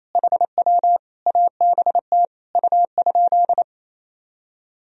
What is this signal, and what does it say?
Morse code, heard as a keyed tone, about 28 wpm, 700 Hz. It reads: HW ABT U?
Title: Sentences from Top 100 Words and Abbreviations 28wpm